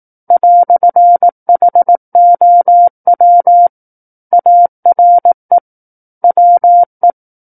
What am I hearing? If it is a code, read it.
Morse code, heard as a keyed tone, about 18 wpm, 700 Hz. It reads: "HOW ARE WE
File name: 18 WPM